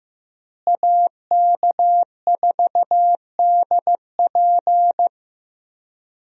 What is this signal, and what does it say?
Morse code, heard as a keyed tone, about 15 wpm, 700 Hz. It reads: AK4DP
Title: USA Call Signs - 5 Characters - Repeated 3x - 15wpm